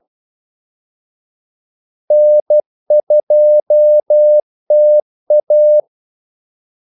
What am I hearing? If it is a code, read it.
Morse code, heard as a keyed tone, about 12 wpm, 600 Hz. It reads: N2TA
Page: 2 ekstra ordmellemrum mellem kaldesignalerne